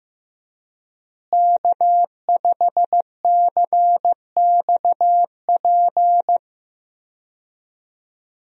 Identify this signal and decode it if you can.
Morse code, heard as a keyed tone, about 15 wpm, 700 Hz. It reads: K5CXP
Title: USA Call Signs - 5 Characters - Repeated 3x - 15wpm